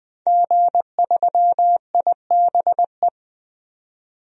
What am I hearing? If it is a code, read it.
Morse code, heard as a keyed tone, about 20 wpm, 700 Hz. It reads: G3IBE
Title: DXCC-100 Call Signs - 5 Characters - 20wpm